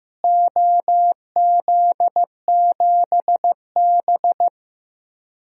Morse code, heard as a keyed tone, about 15 wpm, 700 Hz. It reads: OZ7B
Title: DXCC-100 Call Signs - 4 Characters - Repeated 3x - 15wpm